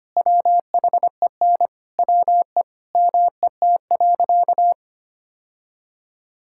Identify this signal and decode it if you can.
Morse code, heard as a keyed tone, about 25 wpm, 700 Hz. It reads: WHEN WE MET.
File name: practice / 25wpm / Sentences with 2-Word Phrases - 25wpm